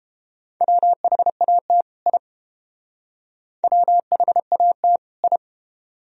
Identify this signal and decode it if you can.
Morse code, heard as a keyed tone, about 29 wpm, 700 Hz. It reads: WHAT I WHAT I